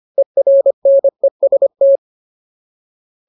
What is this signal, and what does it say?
Morse code, heard as a keyed tone, about 25 wpm, 550 Hz. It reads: ERNEST
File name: Ultimate Word List / 25WPM Ultimate